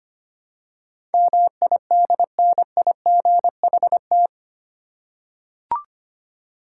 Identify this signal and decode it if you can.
Morse code, heard as a keyed tone, about 25 wpm, 700 Hz. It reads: MIDNIGHT
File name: Eight Letter Words - 25wpm